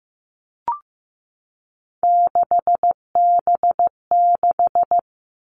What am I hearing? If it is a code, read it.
Morse code, heard as a keyed tone, about 15 wpm, 700 Hz. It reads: 6B6
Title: All Cut Numbers - 3-Digit 15wpm